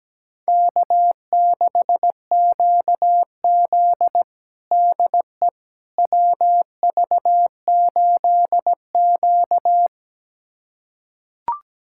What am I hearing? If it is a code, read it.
Morse code, heard as a keyed tone, about 17 wpm, 700 Hz. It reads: K6QZ DE WV8Q
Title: USA Callsign DE Callsign - 4 Characters - 17wpm